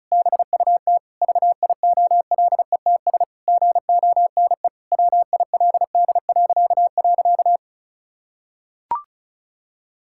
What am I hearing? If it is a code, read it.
Morse code, heard as a keyed tone, about 34 wpm, 700 Hz. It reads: BUT VIOLETS GONE WILD..
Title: Pangram Challenge v2 - 35wpm